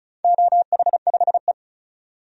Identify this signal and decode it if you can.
Morse code, heard as a keyed tone, about 36 wpm, 700 Hz. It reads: OH5E
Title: DXCC-100 Call Signs - 4 Characters - 35wpm